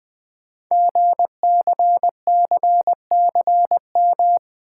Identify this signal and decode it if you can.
Morse code, heard as a keyed tone, about 20 wpm, 700 Hz. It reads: GCCCM
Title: Code Groups - 5 Letters - 20wpm